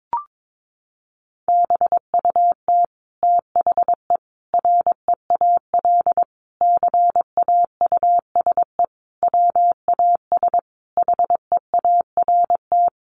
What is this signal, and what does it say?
Morse code, heard as a keyed tone, about 22 wpm, 700 Hz. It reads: BUT THE REAL CAUSE WAS HEART
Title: Sentences from Top 1200 Words 22wpm